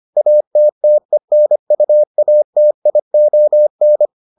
Morse code, heard as a keyed tone, about 25 wpm, 600 Hz. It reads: ATTENUATION